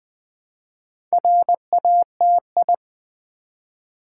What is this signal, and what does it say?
Morse code, heard as a keyed tone, about 20 wpm, 700 Hz. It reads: RATI
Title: Build A Word - 1 Letter At A Time - 5-Letter Words - 20wpm